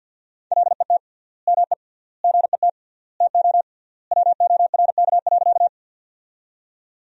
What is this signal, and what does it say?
Morse code, heard as a keyed tone, about 50 wpm, 700 Hz. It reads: LET ME GET TO WORK.
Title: 3x Word-Spacing - Sentences from Top 300 Words - 50wpm